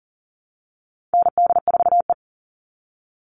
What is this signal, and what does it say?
Morse code, heard as a keyed tone, about 40 wpm, 700 Hz. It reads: ND4E
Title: All Cut Numbers - 4-Digit 40wpm